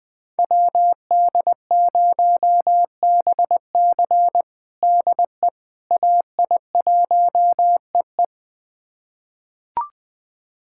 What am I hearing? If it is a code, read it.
Morse code, heard as a keyed tone, about 20 wpm, 700 Hz. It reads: WD0BC DE AI1EE
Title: USA Callsign DE Callsign - 5 Characters - 20wpm